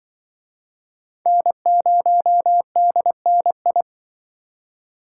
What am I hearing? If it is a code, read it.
Morse code, heard as a keyed tone, about 23 wpm, 700 Hz. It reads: N0DNI
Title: Speed Racing - USA 1x3 Call Signs 20wpm